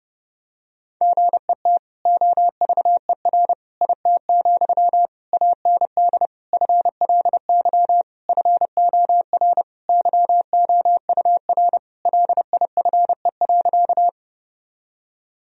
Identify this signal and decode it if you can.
Morse code, heard as a keyed tone, about 30 wpm, 700 Hz. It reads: GET OVER IT, AND FLY FOR YOUR LIFE.
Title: Sentences from Top 1000 Words - Review Part 2 - 30wpm